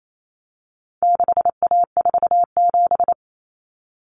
Morse code, heard as a keyed tone, about 28 wpm, 700 Hz. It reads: 6A47
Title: Common Cut Numbers - 4-Digit 28wpm